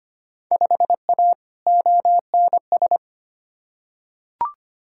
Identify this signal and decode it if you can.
Morse code, heard as a keyed tone, about 25 wpm, 700 Hz. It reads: 5A ONS
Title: Field Day Exchanges Encore - 25wpm